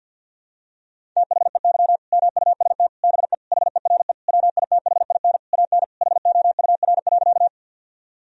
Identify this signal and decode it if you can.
Morse code, heard as a keyed tone, about 50 wpm, 700 Hz. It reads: THEY MUST BE HERE WITHIN AN HOUR.
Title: Sentences from Top 1000 Words - Review Part 4 - 50wpm